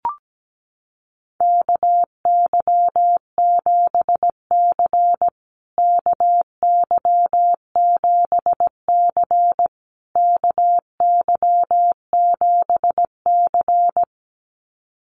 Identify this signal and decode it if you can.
Morse code, heard as a keyed tone, about 17 wpm, 700 Hz. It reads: KY7C KY7C KY7C
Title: USA Call Signs Repeated 3x 17wpm